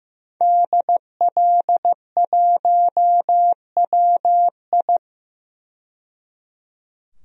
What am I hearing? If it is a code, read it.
Morse code, heard as a keyed tone, about 15 wpm, 700 Hz. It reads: DL1WI
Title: DXCC-100 Call Signs - 5 Characters - 15wpm